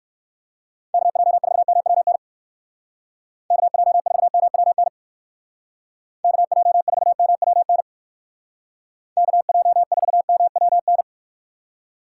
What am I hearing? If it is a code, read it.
Morse code, heard as a keyed tone, about 59 wpm, 700 Hz. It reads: KJ4MWN KJ4MWN KJ4MWN KJ4MWN